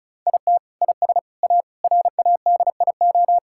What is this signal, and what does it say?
Morse code, heard as a keyed tone, about 36 wpm, 700 Hz. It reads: IT IS A RADIO
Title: Sentences from Top 1000 Words - Review Part 5 - 35wpm